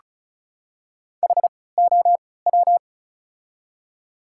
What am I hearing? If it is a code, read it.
Morse code, heard as a keyed tone, about 36 wpm, spaced as for 20 wpm, 700 Hz. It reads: HOW
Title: 3x Character Spacing - Top 100 Words - 35wpm